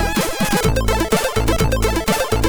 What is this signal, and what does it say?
There is no code.